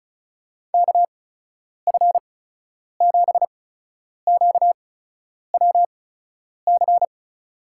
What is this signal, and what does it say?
Morse code, heard as a keyed tone, about 36 wpm, spaced as for 10 wpm, 700 Hz. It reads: KF7QWC
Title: USA Call Signs - 8x Character Spacing - 35wpm